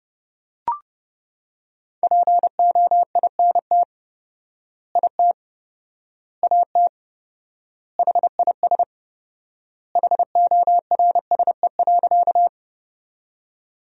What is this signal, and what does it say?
Morse code, heard as a keyed tone, about 30 wpm, 700 Hz. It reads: POINT IT AT HIS HORSE.
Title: 4x Word-Spacing - Sentences from Top 300 Words - 30wpm